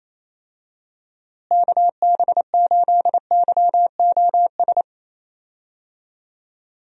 Morse code, heard as a keyed tone, about 28 wpm, 700 Hz. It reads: KB8YOS
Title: USA Call Signs - 6 Characters - Repeated 2x - 28wpm